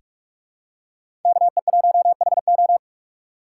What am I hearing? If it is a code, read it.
Morse code, heard as a keyed tone, about 46 wpm, 700 Hz. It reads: KE1HO
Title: USA Call Signs - 5 Characters - 3x Character Spacing - 45wpm